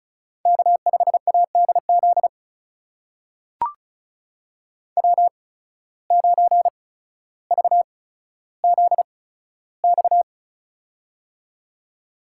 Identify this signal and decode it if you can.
Morse code, heard as a keyed tone, about 34 wpm, spaced as for 15 wpm, 700 Hz. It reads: K5ADZ W9VZX
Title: USA Call Signs - 5 Characters - 8x Character Spacing - 35wpm